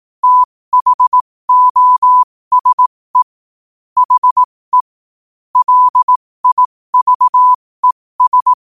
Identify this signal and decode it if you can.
Morse code, heard as a keyed tone, about 18 wpm, spaced as for 15 wpm, 1000 Hz. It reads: THOSE HE LIVES
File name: qotdfiles / 15wpm